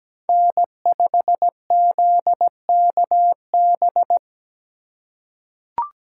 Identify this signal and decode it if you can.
Morse code, heard as a keyed tone, about 17 wpm, 700 Hz. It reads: N5ZKB
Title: USA Call Signs - 5 Characters - 2x Character Spacing - 17wpm